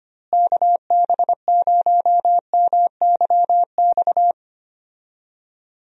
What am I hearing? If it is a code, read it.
Morse code, heard as a keyed tone, about 25 wpm, 700 Hz. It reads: KB0MYX
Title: USA Call Signs - 6 Characters - Repeated 3x - 25wpm